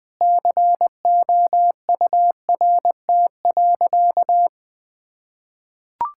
Morse code, heard as a keyed tone, about 21 wpm, 700 Hz. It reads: COURT.
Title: Sentences from Top 1600 Words 20wpm